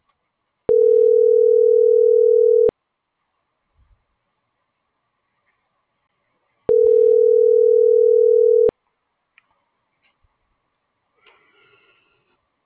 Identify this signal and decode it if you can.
Ringback tone (North America), 440 + 480 Hz.